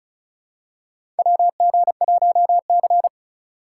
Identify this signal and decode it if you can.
Morse code, heard as a keyed tone, about 34 wpm, 700 Hz. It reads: WG1C
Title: USA Call Signs - 4 Characters - 2x Character Spacing - 35wpm